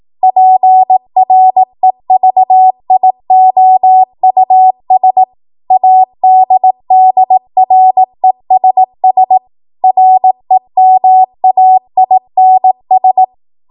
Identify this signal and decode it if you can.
Morse code, heard as a keyed tone, about 18 wpm, 750 Hz. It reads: PREVIOUS ADDRESS REMAINS
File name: Morse / 18WPM